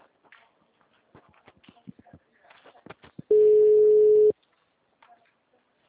Ringback tone (Europe, 425 Hz).